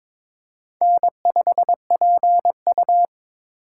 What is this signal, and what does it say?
Morse code, heard as a keyed tone, about 22 wpm, 700 Hz. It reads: N5PU
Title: K1USN SST QSO Practice - 22wpm